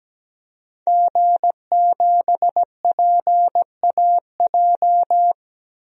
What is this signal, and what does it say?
Morse code, heard as a keyed tone, about 17 wpm, 700 Hz. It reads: G7PAJ
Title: DXCC-100 Call Signs - 5 Characters - 17wpm